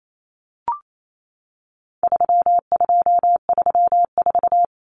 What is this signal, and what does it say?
Morse code, heard as a keyed tone, about 28 wpm, 700 Hz. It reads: 3234
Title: Common Cut Numbers - 4-Digit 28wpm